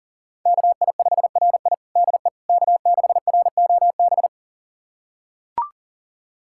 Morse code, heard as a keyed tone, about 40 wpm, 700 Hz. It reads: KI5RI DE K6ROB
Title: USA Callsign DE Callsign - 5 Characters - 40wpm